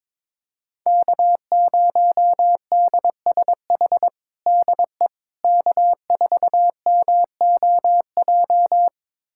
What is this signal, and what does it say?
Morse code, heard as a keyed tone, about 22 wpm, 700 Hz. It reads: K0DSH DE K4MOJ
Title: USA Callsign DE Callsign - 5 Characters - 22wpm